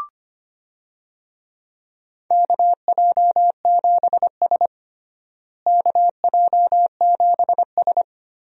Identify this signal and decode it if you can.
Morse code, heard as a keyed tone, about 25 wpm, 700 Hz. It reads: KJ7S KJ7S
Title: USA Call Signs - 4 Characters - Repeated 2x - 25wpm